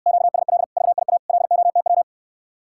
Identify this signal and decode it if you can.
Morse code, heard as a keyed tone, about 69 wpm, 700 Hz. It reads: QSL VIA LOTW